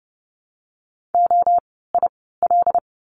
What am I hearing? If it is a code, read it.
Morse code, heard as a keyed tone, about 30 wpm, spaced as for 17 wpm, 700 Hz. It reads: OIL
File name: Lesson 17f - Letter u - Words 30wpm